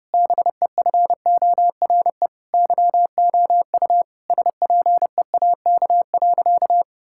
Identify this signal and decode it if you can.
Morse code, heard as a keyed tone, about 30 wpm, 700 Hz. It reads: BEFORE YOU SPEAK.
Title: Sentences from Top 1600 Words 30wpm